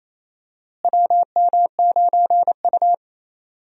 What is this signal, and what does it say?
Morse code, heard as a keyed tone, about 28 wpm, 700 Hz. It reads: WM9U
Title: USA Call Signs - 4 Characters - Repeated 2x - 28wpm